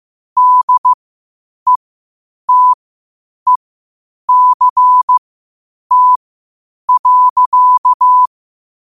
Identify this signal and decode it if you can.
Morse code, heard as a keyed tone, about 15 wpm, spaced as for 8 wpm, 1000 Hz. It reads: DETECT.
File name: qotdfiles / 8wpm